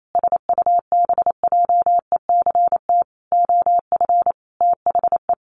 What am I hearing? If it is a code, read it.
Morse code, heard as a keyed tone, about 28 wpm, 700 Hz. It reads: SUBJECT OF THE
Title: Sentences from Top 900 Words 28wpm